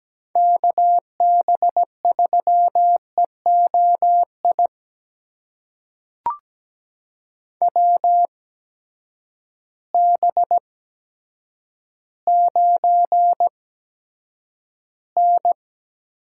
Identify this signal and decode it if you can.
Morse code, heard as a keyed tone, about 17 wpm, spaced as for 9 wpm, 700 Hz. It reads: KB3EOI WB9N
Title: USA Call Signs - 6 Characters - 8x Character Spacing - 17wpm